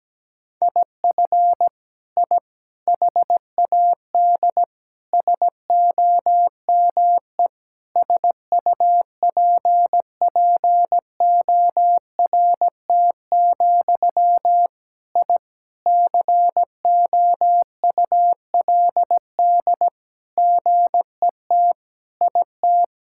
Morse code, heard as a keyed tone, about 17 wpm, 700 Hz. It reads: IF I HAD SOME SUPPORT, I COULD GET IT
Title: Sentences from Top 1000 Words - Review Part 5 - 17wpm